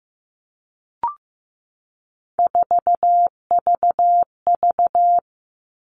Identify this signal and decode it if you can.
Morse code, heard as a keyed tone, about 15 wpm, 700 Hz. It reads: 4VV
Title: All Cut Numbers - 3-Digit 15wpm